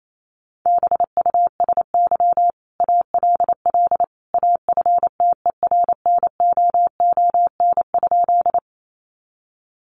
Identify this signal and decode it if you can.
Morse code, heard as a keyed tone, about 28 wpm, 700 Hz. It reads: BUSY ALL AFTERNOON?
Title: Sentences from Top 1500 Words 28wpm